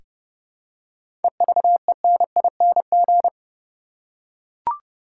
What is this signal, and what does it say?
Morse code, heard as a keyed tone, about 29 wpm, 700 Hz. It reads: EVENING
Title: Seven Letter Words - 30wpm